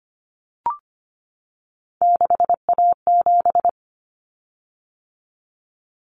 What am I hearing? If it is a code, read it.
Morse code, heard as a keyed tone, about 25 wpm, 700 Hz. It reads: 6A7
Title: Common Cut Numbers - 3-Digit 25wpm